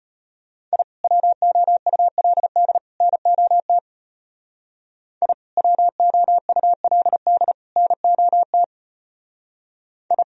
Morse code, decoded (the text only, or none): I WOULD NOT I WOULD NOT I